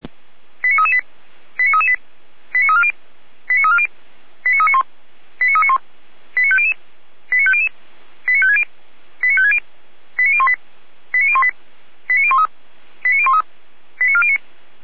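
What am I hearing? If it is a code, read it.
Five-tone selective call (ZVEI), heard as quick runs of five tones, tones of about 70 ms. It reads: 89338 89338 89340 89340 89391 89391 89500 89500 89560 89560 89018 89018 89013 89013 89409
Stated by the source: ZVEI five-tone selective call received on an SDR.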